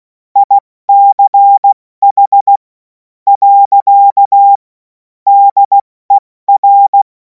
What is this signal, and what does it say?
Morse code, heard as a keyed tone, about 16 wpm, spaced as for 12 wpm, 800 Hz. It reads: ICH. DER
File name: Morsekurs / Witze / Witz 05. Mai 2026 70 F80